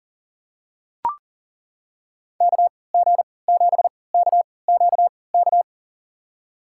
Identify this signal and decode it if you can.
Morse code, heard as a keyed tone, about 40 wpm, spaced as for 23 wpm, 700 Hz. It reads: KG7KQK
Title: USA Call Signs - 3x Character Spacing - 40wpm